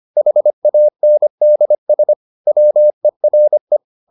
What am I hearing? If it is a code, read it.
Morse code, heard as a keyed tone, about 25 wpm, 600 Hz. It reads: HANDS WERE